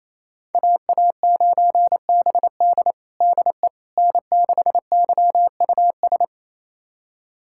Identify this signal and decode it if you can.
Morse code, heard as a keyed tone, about 28 wpm, 700 Hz. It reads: AA9BD DE N6YUS
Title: USA Callsign DE Callsign 28wpm